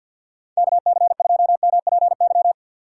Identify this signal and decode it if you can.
Morse code, heard as a keyed tone, about 46 wpm, 700 Hz. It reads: KC2MPY
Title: USA Call Signs - 2x Character Spacing - 50wpm